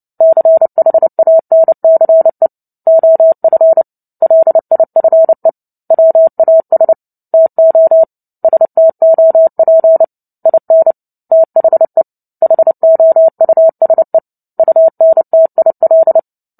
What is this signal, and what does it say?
Morse code, heard as a keyed tone, about 30 wpm, 650 Hz. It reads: CHANCE OF LIFE WAS TO STOP IN THE HOUSE UNTIL
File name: Book Two / 03-Chapter 3 - 29 WPM